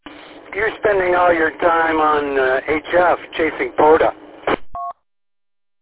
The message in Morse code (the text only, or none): none